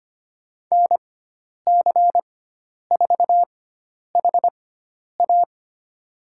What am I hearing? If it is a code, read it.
Morse code, heard as a keyed tone, about 25 wpm, spaced as for 10 wpm, 700 Hz. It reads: NC4HA
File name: Lesson 16g - Letter c - Call Signs 25wpm